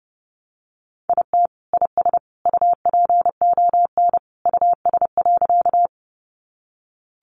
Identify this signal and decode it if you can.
Morse code, heard as a keyed tone, about 28 wpm, 700 Hz. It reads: IT IS UPON US.
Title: Sentences from Top 1100 Words 30wpm